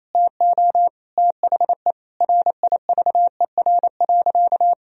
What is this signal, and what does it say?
Morse code, heard as a keyed tone, about 28 wpm, 700 Hz. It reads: TO THE RIVER.
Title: Sentences from Top 1000 Words - Review Part 1 - 28wpm